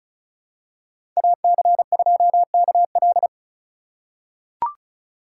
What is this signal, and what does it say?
Morse code, heard as a keyed tone, about 36 wpm, 700 Hz. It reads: AC2KL
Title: USA Call Signs - 4x Character Spacing - 35wpm